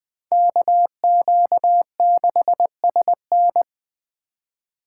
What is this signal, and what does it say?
Morse code, heard as a keyed tone, about 21 wpm, 700 Hz. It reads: KQ6SN